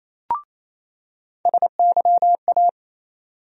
Morse code, heard as a keyed tone, about 28 wpm, 700 Hz. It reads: SYA
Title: USA Call Sign Postfixes - 3 Letters 28wpm